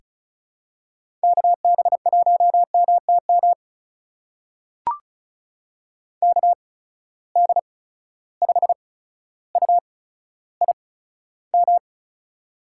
Morse code, heard as a keyed tone, about 36 wpm, spaced as for 15 wpm, 700 Hz. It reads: KB1MTM KD5UIM
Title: USA Call Signs - 6 Characters - 8x Character Spacing - 35wpm